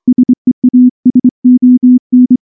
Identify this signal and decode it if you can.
Morse code, heard as a keyed tone, about 23 wpm, 250 Hz. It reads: SEASON
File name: morse / morse-training-25WPM